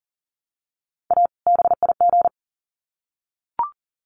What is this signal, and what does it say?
Morse code, heard as a keyed tone, about 40 wpm, 700 Hz. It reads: A BIG
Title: Sets of 2 Words 40wpm